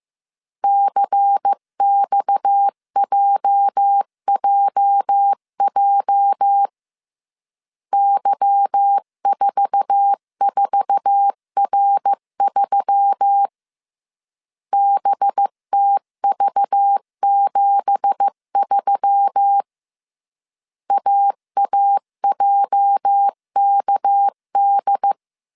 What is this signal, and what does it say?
Morse code, heard as a keyed tone, about 15 wpm, 800 Hz. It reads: CXJJJ Y44R3 BTV73 AAJKD
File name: CW / DET / 1 9 1500C S B 14wpm 3302